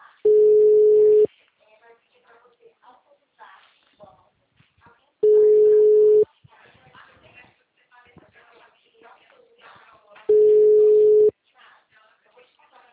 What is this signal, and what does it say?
Ringback tone (Europe, 425 Hz), with speech.